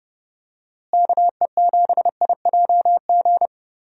Morse code, heard as a keyed tone, about 30 wpm, 700 Hz. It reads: KE7IJG
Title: USA Call Signs - 6 Characters - Repeated 2x - 30wpm